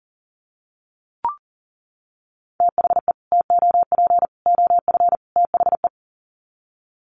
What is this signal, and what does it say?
Morse code, heard as a keyed tone, about 40 wpm, 700 Hz. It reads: THE TOP OF THE
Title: Sets of 4 Words 40wpm